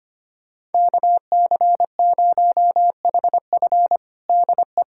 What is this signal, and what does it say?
Morse code, heard as a keyed tone, about 25 wpm, 700 Hz. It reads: KC0HF DE
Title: USA Callsign DE Callsign - 5 Characters - 25wpm